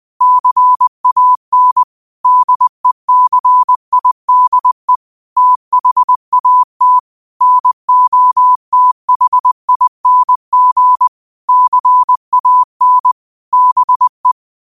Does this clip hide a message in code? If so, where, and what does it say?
Yes: Morse code from 0.2 to 14.3 s, heard as a keyed tone, about 20 wpm, 1000 Hz. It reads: CAN DECIDE THAT NOTHING CAN BE